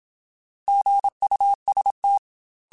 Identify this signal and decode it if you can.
Morse code, heard as a keyed tone, about 27 wpm, 800 Hz. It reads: GUST
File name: Alphabetical / Alphabetical mp3s / Gs @25